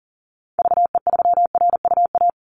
Morse code, heard as a keyed tone, about 41 wpm, 700 Hz. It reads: VE3RUA
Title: Call Signs - 40wpm